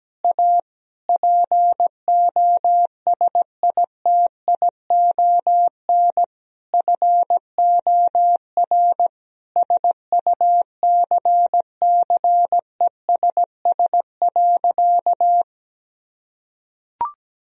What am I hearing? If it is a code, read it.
Morse code, heard as a keyed tone, about 17 wpm, 700 Hz. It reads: A POSITION FOR SUCCESS.
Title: Sentences from Top 1000 Words - Review Part 4 - 17wpm